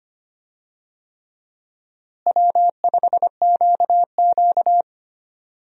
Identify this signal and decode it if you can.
Morse code, heard as a keyed tone, about 25 wpm, 700 Hz. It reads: W5QQ
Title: USA Call Signs - 4 Characters - Repeated 3x - 25wpm